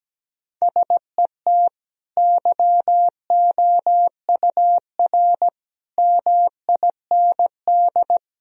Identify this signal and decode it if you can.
Morse code, heard as a keyed tone, about 17 wpm, 700 Hz. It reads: SET YOUR MIND
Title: Sentences from Top 1000 Words - Review Part 1 - 17wpm